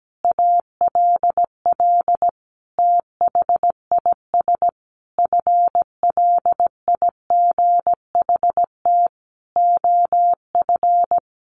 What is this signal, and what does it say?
Morse code, heard as a keyed tone, about 17 wpm, 700 Hz. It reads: ALL THIS FLIGHT OF
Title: Sentences from Top 1300 Words 17wpm